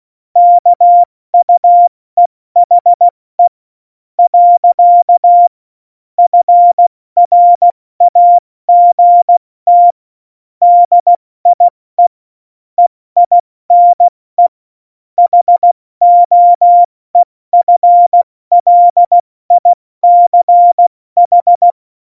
Morse code, heard as a keyed tone, about 16 wpm, 700 Hz. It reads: KUEHE . FRAGT DIE EINE HOEFLICH
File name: Morsekurs / Witze / Witz 19. Juni 2018, 70 F80